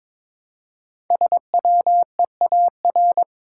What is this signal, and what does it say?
Morse code, heard as a keyed tone, about 22 wpm, 700 Hz. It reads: SWEAR
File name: 5 Letter Words - First 3 Letters Given - 22wpm